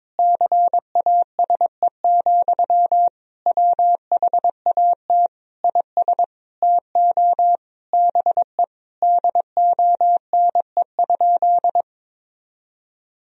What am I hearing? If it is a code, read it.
Morse code, heard as a keyed tone, about 22 wpm, 700 Hz. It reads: CASE, WHAT IS TO BE DONE?
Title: Sentences from Top 1000 Words - Review Part 4 - 22wpm